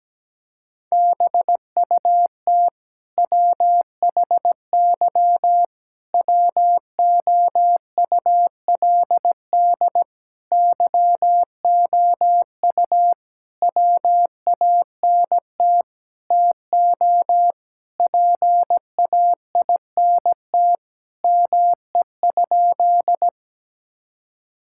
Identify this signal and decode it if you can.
Morse code, heard as a keyed tone, about 17 wpm, 700 Hz. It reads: BUT WHY WOULD YOU WANT TO PAINT ME?
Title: Sentences from Top 1000 Words - Review Part 6 - 17wpm